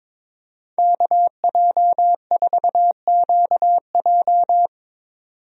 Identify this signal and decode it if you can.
Morse code, heard as a keyed tone, about 22 wpm, 700 Hz. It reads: KJ4QJ
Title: USA Call Signs - 5 Characters - 22wpm